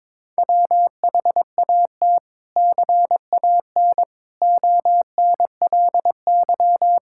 Morse code, heard as a keyed tone, about 21 wpm, 700 Hz. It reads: WHAT CAN ONLY
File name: practice / 22wpm / Sets of 7 Words - 22wpm